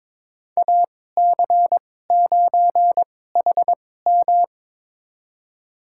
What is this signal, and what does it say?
Morse code, heard as a keyed tone, about 22 wpm, spaced as for 16 wpm, 700 Hz. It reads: AC9HM
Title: USA Call Signs - 2x Character Spacing - 22wpm